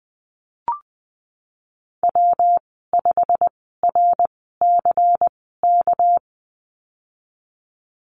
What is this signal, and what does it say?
Morse code, heard as a keyed tone, about 20 wpm, spaced as for 15 wpm, 700 Hz. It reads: W5RCK
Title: 231 Lesson 37g - Letter x - Call Signs 20wpm